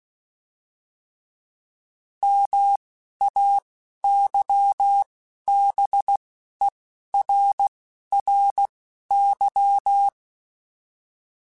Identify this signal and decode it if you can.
Morse code, heard as a keyed tone, about 16 wpm, spaced as for 11 wpm, 800 Hz. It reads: MAYBERRY